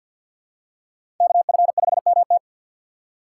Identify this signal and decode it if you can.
Morse code, heard as a keyed tone, about 51 wpm, 700 Hz. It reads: KF5MT
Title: USA Call Signs - 5 Characters - 3x Character Spacing - 50wpm